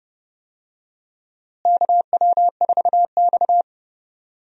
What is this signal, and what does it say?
Morse code, heard as a keyed tone, about 29 wpm, 700 Hz. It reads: KW4X
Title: USA Call Signs - 4 Characters - Repeated 2x - 30wpm